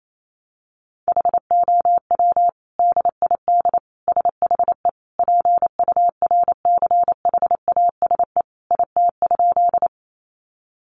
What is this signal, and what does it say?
Morse code, heard as a keyed tone, about 28 wpm, 700 Hz. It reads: HOW DID SHE PURCHASE IT?
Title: Sentences from Top 800 Words 28wpm